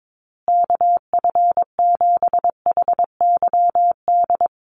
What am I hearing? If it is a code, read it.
Morse code, heard as a keyed tone, about 22 wpm, 700 Hz. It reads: KF7HYD
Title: USA Call Signs Repeated 2x 22wpm